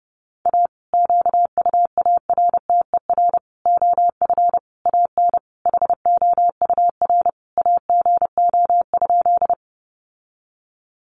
Morse code, heard as a keyed tone, about 30 wpm, 700 Hz. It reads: A QUARTER OF AN HOUR AGO?
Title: Sentences from Top 1300 Words 30wpm